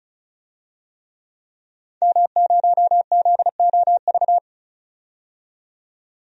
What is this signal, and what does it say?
Morse code, heard as a keyed tone, about 34 wpm, 700 Hz. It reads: M0ZOV
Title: DXCC-100 Call Signs - 5 Characters - 35wpm